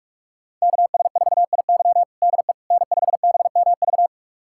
Morse code, heard as a keyed tone, about 46 wpm, 700 Hz. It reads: KS4IY DE N5BMV
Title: USA Callsign DE Callsign - 5 Characters - 45wpm